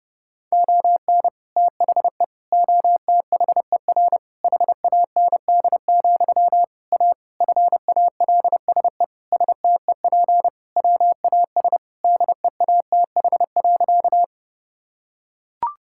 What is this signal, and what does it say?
Morse code, heard as a keyed tone, about 30 wpm, 700 Hz. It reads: ON THE OTHER HAND, A FALSE STEP WAS DEATH.
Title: Sentences from Top 1600 Words 30wpm